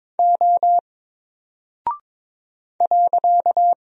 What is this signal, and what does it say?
Morse code, heard as a keyed tone, about 22 wpm, 700 Hz. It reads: O .
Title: Single Character (Letter:Number:Punctuation:Prosign) 22wpm